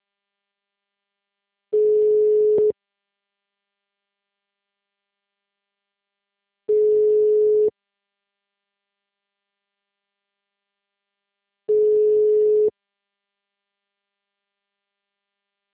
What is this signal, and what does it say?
Ringback tone (Europe, 425 Hz).
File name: out- -2010- - - .18